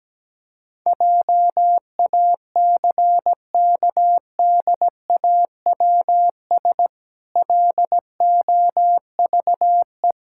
Morse code, heard as a keyed tone, about 17 wpm, 700 Hz. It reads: JACKDAWS LOVE
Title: Pangram Challenge v2 - 17wpm